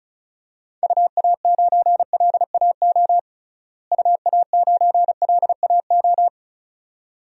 Morse code, heard as a keyed tone, about 36 wpm, 700 Hz. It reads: UA9LAO UA9LAO
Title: DXCC-100 Call Signs - 6 Characters - Repeated 2x - 35wpm